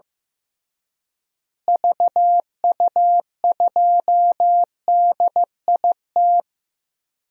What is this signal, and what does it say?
Morse code, heard as a keyed tone, about 15 wpm, 700 Hz. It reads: VU2DIT
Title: DXCC-100 Call Signs - 6 Characters - Repeated 3x - 15wpm